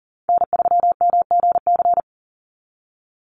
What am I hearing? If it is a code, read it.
Morse code, heard as a keyed tone, about 40 wpm, 700 Hz. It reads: N3MGC
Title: USA Call Signs 40wpm